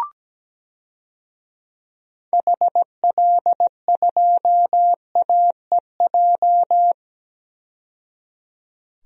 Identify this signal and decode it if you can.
Morse code, heard as a keyed tone, about 17 wpm, 700 Hz. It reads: HL2AEJ
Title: DXCC-100 Call Signs - 6 Characters - 17wpm